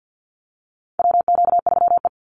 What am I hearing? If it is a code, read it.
Morse code, heard as a keyed tone, about 50 wpm, 700 Hz. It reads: WQ3E
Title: Call Signs Repeated 3x - 50wpm